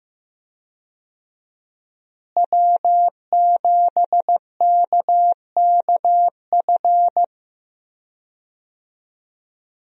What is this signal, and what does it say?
Morse code, heard as a keyed tone, about 15 wpm, 700 Hz. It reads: W7KKF
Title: USA Call Signs - 5 Characters - Repeated 2x - 15wpm